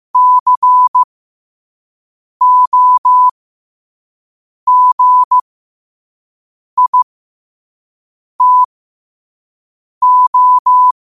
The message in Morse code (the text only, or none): COGITO